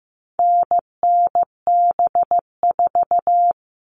Morse code, heard as a keyed tone, about 15 wpm, 700 Hz. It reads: NNB4
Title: All Cut Numbers - 4-Digit 15wpm